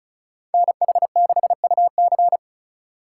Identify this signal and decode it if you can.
Morse code, heard as a keyed tone, about 34 wpm, 700 Hz. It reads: NH6UC
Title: USA Call Signs - 5 Characters - 8x Character Spacing - 35wpm